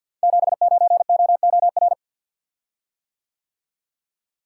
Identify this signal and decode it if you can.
Morse code, heard as a keyed tone, about 46 wpm, 700 Hz. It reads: 79OOR
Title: Code Groups - 5 Letter:Numbers - 50wpm